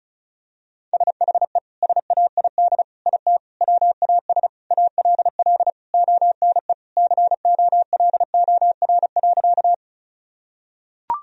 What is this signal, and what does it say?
Morse code, heard as a keyed tone, about 36 wpm, 700 Hz. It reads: SHE SAID IT WAS ALL ONE COLOR.
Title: Sentences from Top 1000 Words - Review Part 2 - 35wpm